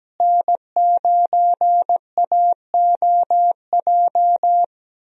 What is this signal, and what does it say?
Morse code, heard as a keyed tone, about 17 wpm, 700 Hz. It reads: N9AOJ
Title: USA Call Signs - 5 Characters - Repeated 3x - 17wpm